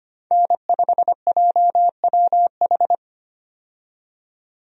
Morse code, heard as a keyed tone, about 25 wpm, 690 Hz. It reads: N5JWH
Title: USA Call Signs - 5 Characters - 25wpm